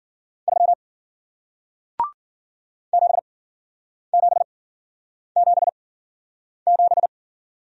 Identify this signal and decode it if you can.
Morse code, heard as a keyed tone, about 53 wpm, spaced as for 9 wpm, 700 Hz. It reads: 3 7777